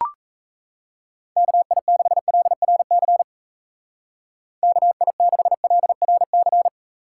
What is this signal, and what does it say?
Morse code, heard as a keyed tone, about 35 wpm, 700 Hz. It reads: KI6LRC KI6LRC